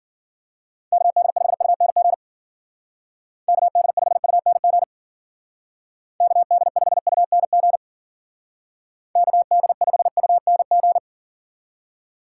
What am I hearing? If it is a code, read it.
Morse code, heard as a keyed tone, about 52 wpm, 700 Hz. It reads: KD5UNG KD5UNG KD5UNG KD5UNG